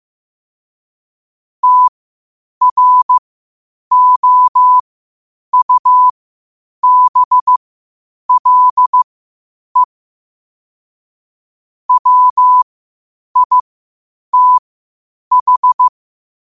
Morse code, heard as a keyed tone, about 15 wpm, spaced as for 8 wpm, 1000 Hz. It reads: TROUBLE WITH
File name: 8wpm